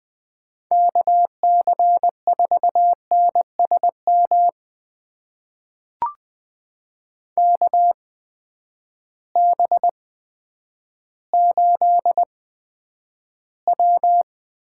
Morse code, heard as a keyed tone, about 20 wpm, spaced as for 10 wpm, 700 Hz. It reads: KC4NSM KB8W
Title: USA Call Signs - 6 Characters - 8x Character Spacing - 20wpm